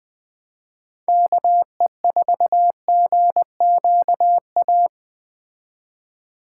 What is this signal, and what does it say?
Morse code, heard as a keyed tone, about 20 wpm, 700 Hz. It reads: KE4GQA